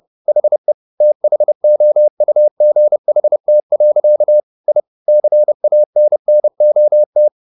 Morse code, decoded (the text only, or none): HE THOUGHT. I CANNOT